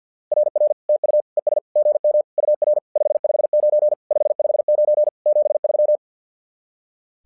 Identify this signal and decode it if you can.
Morse code, heard as a keyed tone, about 50 wpm, 600 Hz. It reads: RR TU ES GM UR 559 559 73
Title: SOTA QSO Practice 50wpm